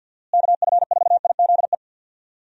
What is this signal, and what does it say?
Morse code, heard as a keyed tone, about 52 wpm, 700 Hz. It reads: KF4IZE
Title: USA Call Signs - 6 Characters - Repeated 2x - 50wpm